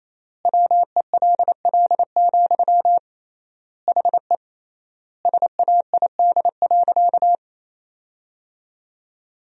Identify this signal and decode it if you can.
Morse code, heard as a keyed tone, about 28 wpm, 700 Hz. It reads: WELL, HE SAID.
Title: 3x Word-Spacing - Sentences from Top 200 Words - 28wpm